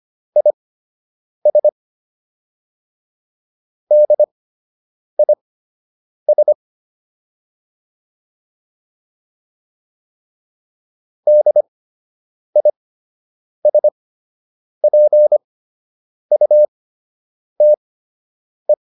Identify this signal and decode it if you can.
Morse code, heard as a keyed tone, about 25 wpm, spaced as for 8 wpm, 600 Hz. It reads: IS DIS DISPUTE